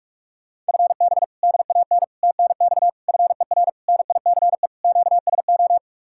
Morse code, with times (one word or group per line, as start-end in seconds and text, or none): 0.7-1.2 FB
1.4-2.0 DAN
2.2-2.9 TNX
3.1-3.7 FER
3.9-4.7 NICE
4.8-5.8 QSO